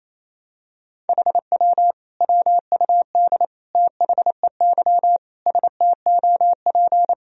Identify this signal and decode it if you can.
Morse code, heard as a keyed tone, about 28 wpm, 700 Hz. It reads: HW WUD THEY STOP
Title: Sentences from Top 300 Words and Abbreviations 28wpm